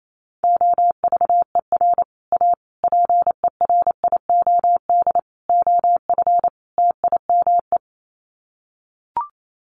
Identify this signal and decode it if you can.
Morse code, heard as a keyed tone, about 28 wpm, 700 Hz. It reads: OVER A PERIOD OF TIME
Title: Sets of 5 Words Encore2 28wpm